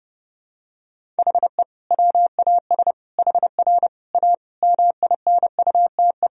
Morse code, heard as a keyed tone, about 30 wpm, 700 Hz. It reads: HE WAS HR A MINUTE
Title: Sentences from Top 500 Words and Abbreviations 30wpm